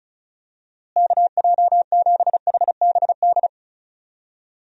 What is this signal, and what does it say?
Morse code, heard as a keyed tone, about 36 wpm, 700 Hz. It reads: KJ7HBD
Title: USA Call Signs - 6 Characters - Repeated 2x - 35wpm